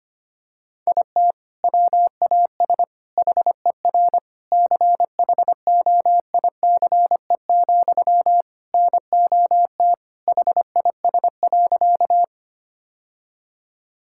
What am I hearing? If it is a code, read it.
Morse code, heard as a keyed tone, about 25 wpm, 700 Hz. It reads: IT WAS HER CHOICE, NOT HIS.
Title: Sentences from Top 1000 Words - Review Part 6 - 25wpm